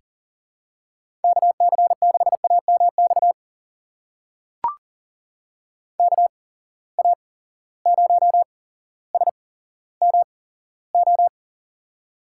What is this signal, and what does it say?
Morse code, heard as a keyed tone, about 40 wpm, spaced as for 17 wpm, 700 Hz. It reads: KC6AMX KA0SMO
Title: USA Call Signs - 6 Characters - 8x Character Spacing - 40wpm